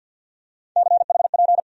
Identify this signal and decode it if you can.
Morse code, heard as a keyed tone, about 50 wpm, 700 Hz. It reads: CHP
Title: USA Call Sign Postfixes - 3 Letters 50wpm